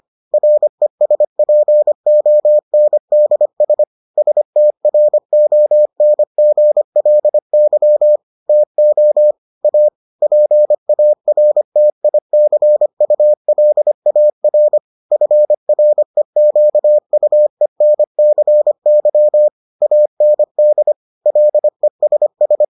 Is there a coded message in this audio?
Yes: Morse code from 0.3 to 22.6 s, heard as a keyed tone, about 25 wpm, 600 Hz. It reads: RESPONDS STRONGLY TO A PARTICULAR FREQUENCY AND LESS